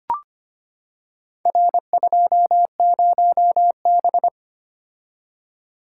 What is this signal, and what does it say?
Morse code, heard as a keyed tone, about 25 wpm, 700 Hz. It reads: R20B
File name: practice / 25wpm / Four Letter-Number Combinations - 25wpm